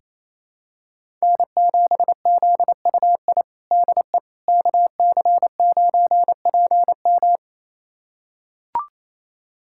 Morse code, heard as a keyed tone, about 28 wpm, 700 Hz. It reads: N7ZUI DE KC9PM
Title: USA Callsign DE Callsign - 5 Characters - 28wpm